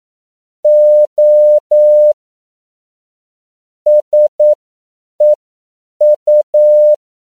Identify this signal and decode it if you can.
Morse code, heard as a keyed tone, about 9 wpm, spaced as for 7 wpm, 600 Hz. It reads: O SEU
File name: CW / 10PPM